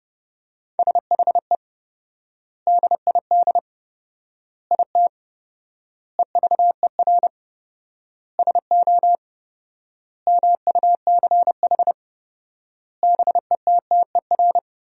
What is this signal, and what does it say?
Morse code, heard as a keyed tone, about 32 wpm, 700 Hz. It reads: SHE DID IT EVER SO MUCH BETTER
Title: 4x Word-Spacing - Sentences from Top 400 Words - 30wpm